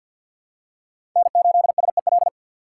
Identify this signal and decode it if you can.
Morse code, heard as a keyed tone, about 52 wpm, 700 Hz. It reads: N8SEL
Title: USA Call Signs - 5 Characters - 6x Character Spacing - 50wpm